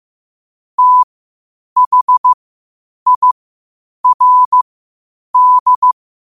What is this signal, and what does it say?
Morse code, heard as a keyed tone, about 15 wpm, spaced as for 8 wpm, 1000 Hz. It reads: THIRD
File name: 8wpm